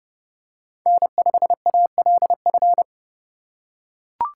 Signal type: Morse code

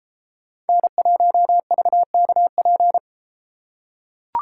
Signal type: Morse code

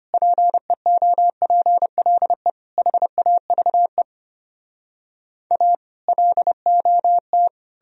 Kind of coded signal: Morse code

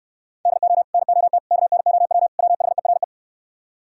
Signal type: Morse code

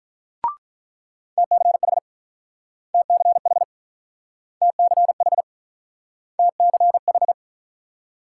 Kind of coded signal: Morse code

